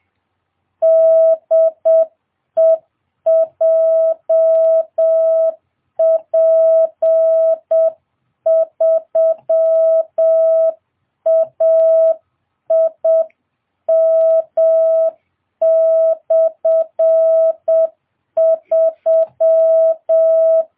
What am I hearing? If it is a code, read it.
Morse code, heard as a keyed tone, about 7 wpm, 660 Hz. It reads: DE JP3AIM/3